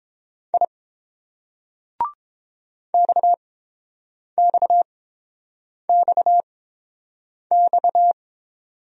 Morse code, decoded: I XXXX

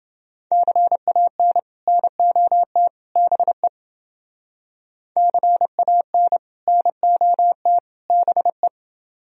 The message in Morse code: CAN NOT BE CAN NOT BE

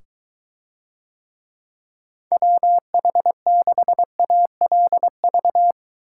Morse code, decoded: WH6ALV

